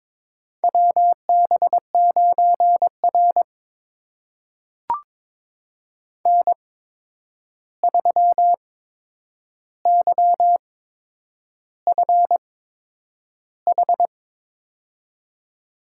Morse code, heard as a keyed tone, about 22 wpm, spaced as for 9 wpm, 700 Hz. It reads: WB9R N3YFH
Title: USA Call Signs - 8x Character Spacing - 22wpm